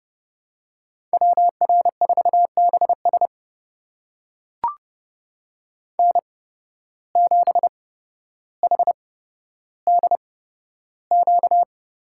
Morse code, heard as a keyed tone, about 30 wpm, spaced as for 13 wpm, 700 Hz. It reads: WR4BS N7HDQ
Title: USA Call Signs - 5 Characters - 8x Character Spacing - 30wpm